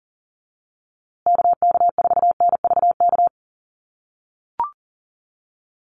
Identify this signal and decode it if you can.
Morse code, heard as a keyed tone, about 40 wpm, 700 Hz. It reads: KK4NVK
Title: USA Call Signs Repeated 2x 40wpm